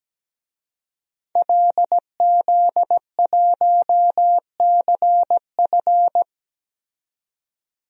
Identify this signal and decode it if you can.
Morse code, heard as a keyed tone, about 17 wpm, 700 Hz. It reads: LZ1CF
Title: DXCC-100 Call Signs - 5 Characters - 17wpm